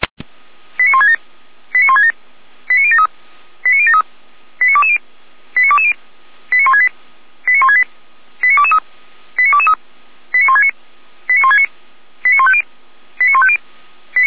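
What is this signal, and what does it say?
Five-tone selective call (ZVEI), heard as quick runs of five tones, tones of about 70 ms. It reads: 89167 89167 89073 89073 89220 89220 89168 89168 89223 89223 89169 89169 89150 89150
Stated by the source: ZVEI five-tone selective call received by SDR, recorded for Feuerwehr Orlamünde (fire brigade).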